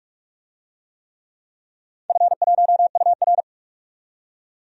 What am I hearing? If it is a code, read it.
Morse code, heard as a keyed tone, about 41 wpm, 700 Hz. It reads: F1UR
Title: DXCC-100 Call Signs - 4 Characters - 45wpm